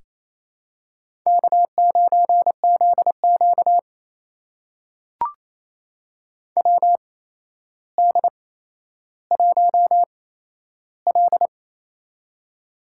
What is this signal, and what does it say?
Morse code, heard as a keyed tone, about 28 wpm, spaced as for 12 wpm, 700 Hz. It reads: K9ZQ WD1L
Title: USA Call Signs - 4 Characters - 8x Character Spacing - 28wpm